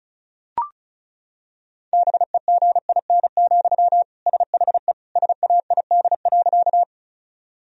Morse code, heard as a keyed tone, about 34 wpm, 700 Hz. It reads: BEGIN, SHE SAID.
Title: Sentences from Top 1000 Words - Review Part 1 - 35wpm